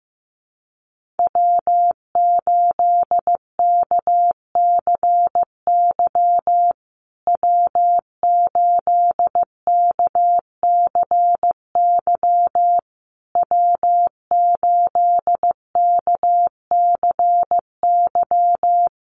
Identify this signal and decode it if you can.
Morse code, heard as a keyed tone, about 15 wpm, 700 Hz. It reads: W8KCY W8KCY W8KCY
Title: USA Call Signs Repeated 3x 15wpm